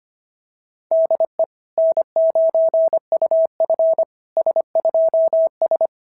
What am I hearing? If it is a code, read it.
Morse code, heard as a keyed tone, about 25 wpm, 690 Hz. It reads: DE N9UF S2S
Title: SOTA QSO Practice 25wpm